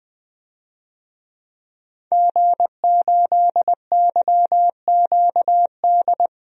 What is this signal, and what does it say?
Morse code, heard as a keyed tone, about 20 wpm, 700 Hz. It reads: G8YQD
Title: DXCC-100 Call Signs - 5 Characters - Repeated 2x - 20wpm